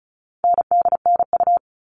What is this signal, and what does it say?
Morse code, heard as a keyed tone, about 34 wpm, 700 Hz. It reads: NDNU
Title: All Cut Numbers - 4-Digit 35wpm